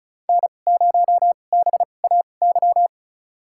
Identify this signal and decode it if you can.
Morse code, heard as a keyed tone, about 36 wpm, spaced as for 26 wpm, 700 Hz. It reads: N0BAY
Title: USA Call Signs - 5 Characters - 2x Character Spacing - 35wpm